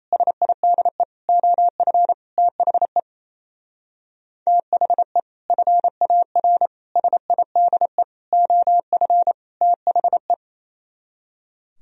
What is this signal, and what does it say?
Morse code, heard as a keyed tone, about 34 wpm, 700 Hz. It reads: SIDE OF THE THE FAR SIDE OF THE
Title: Speed-Racing - Sets of 5 Words 28wpm